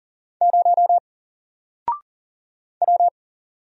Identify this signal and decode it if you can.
Morse code, heard as a keyed tone, about 40 wpm, 700 Hz. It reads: 0 W
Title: Single Character (Letter:Number:Punctuation:Prosign) Alt 40wpm